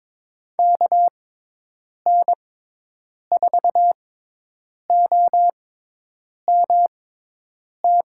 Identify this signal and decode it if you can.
Morse code, heard as a keyed tone, about 22 wpm, spaced as for 8 wpm, 700 Hz. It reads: KN4OMT
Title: USA Call Signs - 6 Characters - 6x Character Spacing - 22wpm